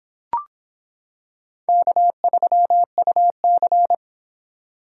Morse code, heard as a keyed tone, about 26 wpm, 700 Hz. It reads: K3UC